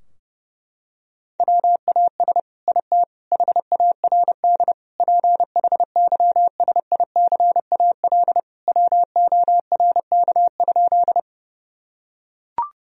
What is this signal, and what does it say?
Morse code, heard as a keyed tone, about 30 wpm, 700 Hz. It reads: WAS IT HARD PHYSICAL WORK?
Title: Sentences from Top 1000 Words - Review Part 4 - 30wpm